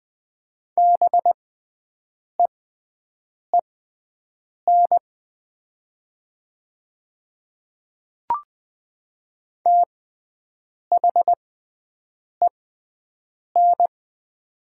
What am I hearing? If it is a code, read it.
Morse code, heard as a keyed tone, about 20 wpm, spaced as for 7 wpm, 700 Hz. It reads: BEEN THEN
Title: 6x Character Spacing - Top 100 Words - 20wpm